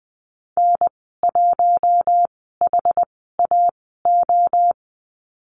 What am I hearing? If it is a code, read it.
Morse code, heard as a keyed tone, about 20 wpm, spaced as for 15 wpm, 700 Hz. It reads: N1HAO
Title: 062 Lesson 11g - Letter h - Call Signs 20wpm